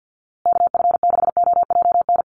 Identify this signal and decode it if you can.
Morse code, heard as a keyed tone, about 50 wpm, 700 Hz. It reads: KF6OJN
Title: USA Call Signs Repeated 2x 50wpm